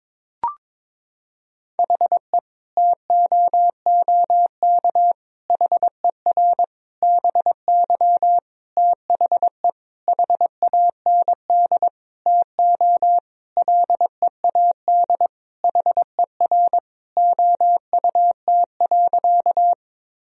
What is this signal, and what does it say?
Morse code, heard as a keyed tone, about 22 wpm, 700 Hz. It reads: HE TOOK HER BY THE HAND TO LEAD HER OUT.
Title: Sentences from Top 1000 Words - Review Part 2 - 22wpm